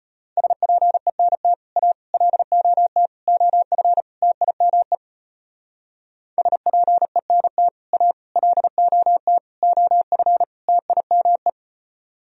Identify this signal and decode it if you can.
Morse code, heard as a keyed tone, about 32 wpm, 700 Hz. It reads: SPENT A LOT OF TIME SPENT A LOT OF TIME